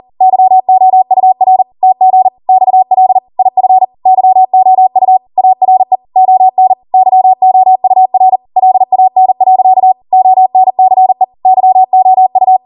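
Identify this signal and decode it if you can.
Morse code, heard as a keyed tone, about 40 wpm, 750 Hz. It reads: YOUR TG XL IF YOU ARE ON YOUR LAN. ONCE YOU
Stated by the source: Code practice at 40 wpm.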